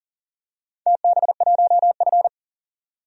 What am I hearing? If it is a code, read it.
Morse code, heard as a keyed tone, about 40 wpm, 700 Hz. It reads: TB1F
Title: DXCC-100 Call Signs - 4 Characters - 40wpm